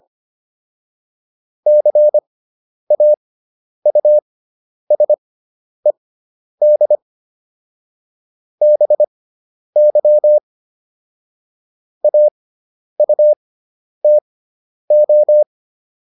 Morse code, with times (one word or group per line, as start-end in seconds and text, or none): 1.7-7.0 CAUSED
8.6-10.4 BY
12.0-15.4 AUTO